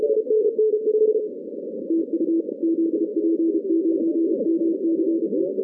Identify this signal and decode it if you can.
Morse code, heard as a keyed tone, about 32 wpm, 330 Hz. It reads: TU ZP9MCE
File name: 2015 cq ww cw / 2015 cqwww k5tr - 0-18